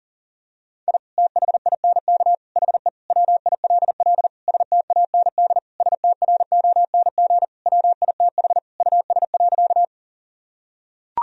Morse code, heard as a keyed tone, about 37 wpm, 700 Hz. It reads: I THINK HE WILL STAND STRONG WITH US.
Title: Sentences from Top 1000 Words - Review Part 3 - 40wpm